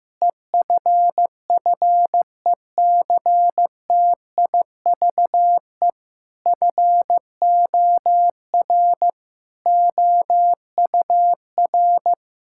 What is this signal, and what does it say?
Morse code, heard as a keyed tone, about 14 wpm, 700 Hz. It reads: EFFECTIVE FOR OUR
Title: Sentences from Top 1000 Words - Review Part 6 - 15wpm